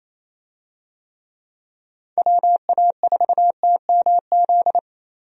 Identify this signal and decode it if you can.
Morse code, heard as a keyed tone, about 28 wpm, 700 Hz. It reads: WA4TMZ
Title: USA Call Signs - 6 Characters - 28wpm